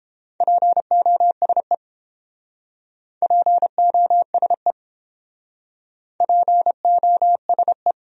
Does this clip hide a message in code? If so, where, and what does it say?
Yes: Morse code from 0.4 to 7.9 s, heard as a keyed tone, about 29 wpm, 700 Hz. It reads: POSE POSE POSE